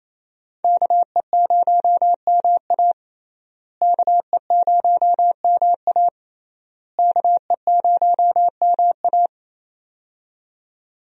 Morse code, heard as a keyed tone, about 28 wpm, 700 Hz. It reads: KE0MA KE0MA KE0MA
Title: USA Call Signs - 5 Characters - Repeated 3x - 28wpm